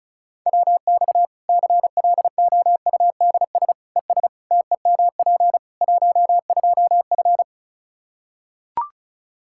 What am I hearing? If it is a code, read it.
Morse code, heard as a keyed tone, about 34 wpm, 700 Hz. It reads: WX CLOUDS ES TEMP 12F
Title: Practice QSO - Exchange 2 - Info - 35wpm